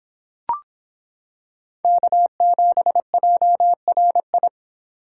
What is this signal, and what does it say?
Morse code, heard as a keyed tone, about 26 wpm, 700 Hz. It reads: K7JRI